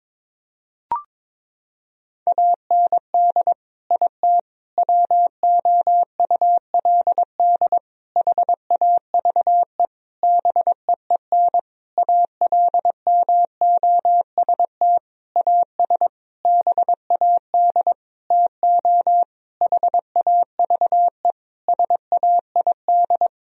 Morse code, heard as a keyed tone, about 22 wpm, 700 Hz. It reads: AND IT WOULD HAVE BEEN ALMOST AS BAD TO HAVE SAID
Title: Sentences from Top 1000 Words - Review Part 4 - 22wpm